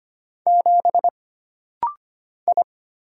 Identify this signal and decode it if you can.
Morse code, heard as a keyed tone, about 25 wpm, 700 Hz. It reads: 7 I
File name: Single Character (Letter:Number:Punctuation:Prosign) - Rapid-Fire - 25wpm